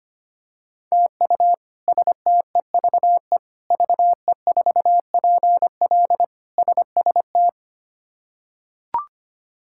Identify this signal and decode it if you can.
Morse code, heard as a keyed tone, about 25 wpm, 700 Hz. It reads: TU STEVE VE4PL SST
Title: K1USN SST QSO Practice - 25wpm